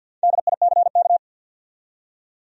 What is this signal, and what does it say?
Morse code, heard as a keyed tone, about 50 wpm, 700 Hz. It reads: DICK
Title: USA Common Names - Max 5 Letters - 50wpm